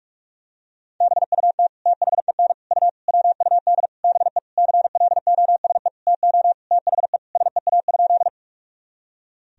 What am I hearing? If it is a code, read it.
Morse code, heard as a keyed tone, about 46 wpm, 700 Hz. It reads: BUT THEN U WUD BE CLOSE TO THE SEA?
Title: Sentences from Top 300 Words and Abbreviations 45wpm